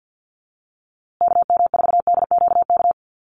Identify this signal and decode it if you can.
Morse code, heard as a keyed tone, about 50 wpm, 700 Hz. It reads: KM4DQK
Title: USA Call Signs Repeated 2x 50wpm